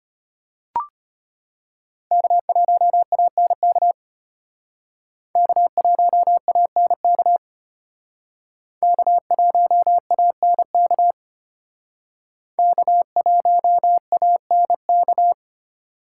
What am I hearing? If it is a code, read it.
Morse code, heard as a keyed tone, about 32 wpm, 700 Hz. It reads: K1ANK K1ANK K1ANK K1ANK